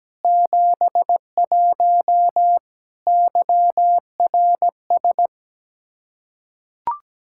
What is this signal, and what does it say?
Morse code, heard as a keyed tone, about 17 wpm, 700 Hz. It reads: 71 YRS
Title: Practice QSO - Exchange 3 - More Info - 17wpm